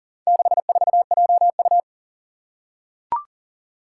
Morse code, heard as a keyed tone, about 40 wpm, 700 Hz. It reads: 64JU
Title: Four Challenging Letter-Number Combinations 40wpm